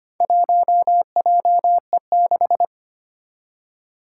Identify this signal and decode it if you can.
Morse code, heard as a keyed tone, about 25 wpm, 700 Hz. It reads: 1JE6
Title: Four Challenging Letter-Number Combinations 25wpm